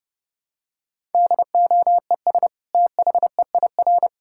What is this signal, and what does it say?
Morse code, heard as a keyed tone, about 30 wpm, 700 Hz. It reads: DOES THEIR
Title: Sentences from Top 1000 Words - Review Part 6 - 30wpm